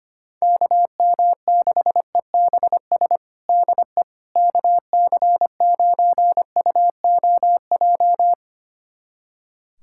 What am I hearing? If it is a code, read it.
Morse code, heard as a keyed tone, about 25 wpm, 700 Hz. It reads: KM6EBS DE KC9UOJ
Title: USA Callsign DE Callsign - 6 Characters - 25wpm